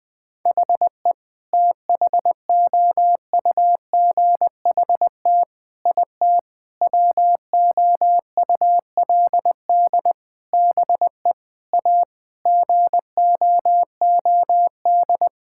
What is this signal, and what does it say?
Morse code, heard as a keyed tone, about 21 wpm, 700 Hz. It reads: HE THOUGHT IT WOULD BE A GOOD
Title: Sentences with 8-Word Phrases - Spoken Beginning - 20wpm